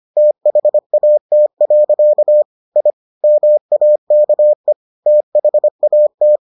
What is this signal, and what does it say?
Morse code, heard as a keyed tone, about 25 wpm, 600 Hz. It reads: THAT. I MAKE THAT